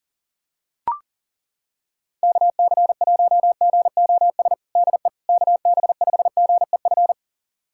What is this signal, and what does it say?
Morse code, heard as a keyed tone, about 37 wpm, 700 Hz. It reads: KC1GOS DE KB5GEF
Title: USA Callsign DE Callsign - 6 Characters - 40wpm